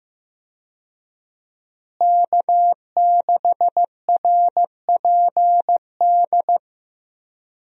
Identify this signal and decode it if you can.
Morse code, heard as a keyed tone, about 15 wpm, 700 Hz. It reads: K6RPD
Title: USA Call Signs - 5 Characters - Repeated 2x - 15wpm